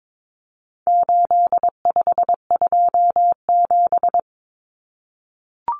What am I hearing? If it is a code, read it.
Morse code, heard as a keyed tone, about 22 wpm, 700 Hz. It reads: 8527
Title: Common Cut Numbers - 4-Digit 22wpm